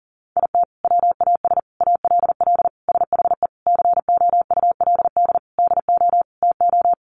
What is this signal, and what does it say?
Morse code, heard as a keyed tone, about 40 wpm, 700 Hz. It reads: IT WAS ALL SHE COULD DO TO
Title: Sentences from Top 400 Words 40wpm